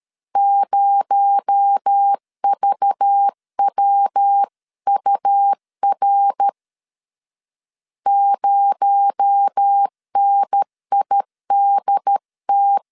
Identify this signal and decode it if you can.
Morse code, heard as a keyed tone, about 13 wpm, 800 Hz. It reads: 0VWUR 0NIDT
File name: LEZIONI / 6 PKX0 / 6 PKX0 12wpm 2231